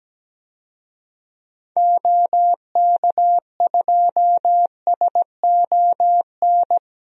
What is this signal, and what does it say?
Morse code, heard as a keyed tone, about 17 wpm, 700 Hz. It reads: OK2SON
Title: DXCC-100 Call Signs - 6 Characters - 17wpm